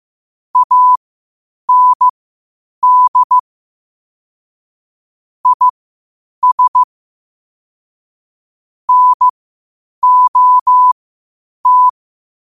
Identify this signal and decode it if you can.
Morse code, heard as a keyed tone, about 15 wpm, spaced as for 8 wpm, 1000 Hz. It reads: AND IS NOT
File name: qotdfiles / 8wpm